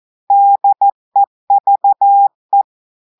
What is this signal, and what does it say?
Morse code, heard as a keyed tone, about 14 wpm, 800 Hz. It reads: DEVE
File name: audios / portugais / portugais texte en clair 2 vitesse 14 mots minutes